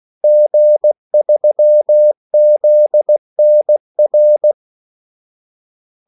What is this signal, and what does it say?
Morse code, heard as a keyed tone, about 16 wpm, 600 Hz. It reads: G3ZNR
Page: Kaldesignaler 16 wpm